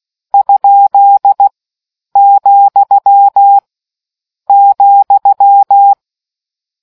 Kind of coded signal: Morse code